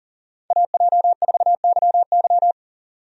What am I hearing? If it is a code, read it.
Morse code, heard as a keyed tone, about 40 wpm, 700 Hz. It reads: AJ4YY
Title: USA Call Signs - 5 Characters - 40wpm